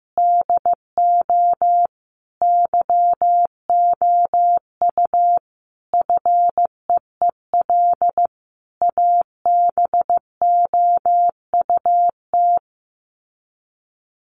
Morse code, heard as a keyed tone, about 15 wpm, 700 Hz. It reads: DO YOU FEEL ABOUT
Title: Sets of 5 Words 15wpm